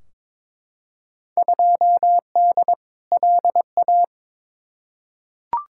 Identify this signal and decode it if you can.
Morse code, heard as a keyed tone, about 22 wpm, 700 Hz. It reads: 2D LA
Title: Field Day Exchanges - 22wpm